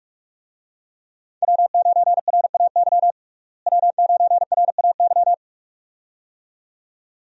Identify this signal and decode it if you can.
Morse code, heard as a keyed tone, about 46 wpm, 700 Hz. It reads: W9RAY W9RAY
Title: USA Call Signs - 5 Characters - Repeated 2x - 45wpm